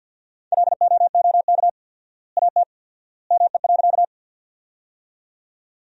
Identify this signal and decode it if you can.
Morse code, heard as a keyed tone, about 47 wpm, 700 Hz. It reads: LOOK AT ME.
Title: 4x Word-Spacing - Sentences from Top 200 Words - 50wpm